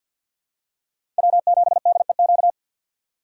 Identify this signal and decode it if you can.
Morse code, heard as a keyed tone, about 53 wpm, 700 Hz. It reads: W7DEQ
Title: USA Call Signs - 5 Characters - 2x Character Spacing - 50wpm